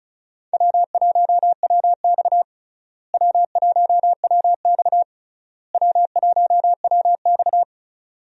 Morse code, heard as a keyed tone, about 36 wpm, 700 Hz. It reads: W1WX W1WX W1WX
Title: USA Call Signs - 4 Characters - Repeated 3x - 35wpm